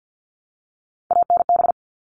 Morse code, heard as a keyed tone, about 50 wpm, 700 Hz. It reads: ANB
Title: All Cut Numbers - 3-Digit 50wpm